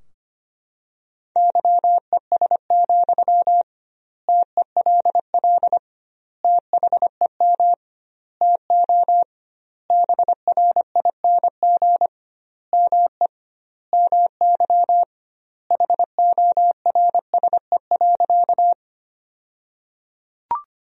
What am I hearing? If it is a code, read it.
Morse code, heard as a keyed tone, about 25 wpm, 700 Hz. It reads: YES, TELL THEM TO BRING ME MY HORSE.
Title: 2x Word-Spacing - Sentences from Top 500 Words - 25wpm